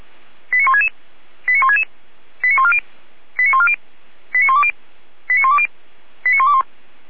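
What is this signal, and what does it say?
Five-tone selective call (ZVEI), heard as quick runs of five tones, tones of about 70 ms. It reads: 89166 89166 89140 89140 89120 89120 89121
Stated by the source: ZVEI five-tone selective call 89119, received by SDR.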